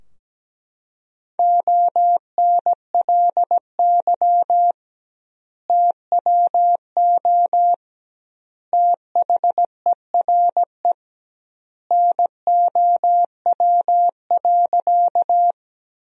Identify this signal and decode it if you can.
Morse code, heard as a keyed tone, about 17 wpm, 700 Hz. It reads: ONLY TWO THERE NOW.
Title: 2x Word-Spacing - Sentences from Top 200 Words - 17wpm